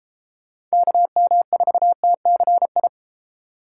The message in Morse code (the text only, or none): KM4TCI